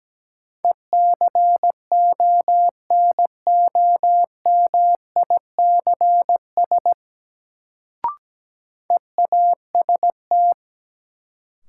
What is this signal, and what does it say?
Morse code, heard as a keyed tone, about 17 wpm, 700 Hz. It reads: ECONOMICS EAST
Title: Sentences from Top 1000 Words - Review Part 3 - New Word Drill - 17wpm